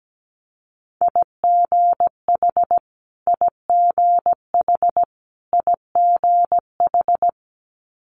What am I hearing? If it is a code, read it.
Morse code, heard as a keyed tone, about 17 wpm, 700 Hz. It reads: IGH IGH IGH
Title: Multi-letter Phonemes Repeated 3x 17wpm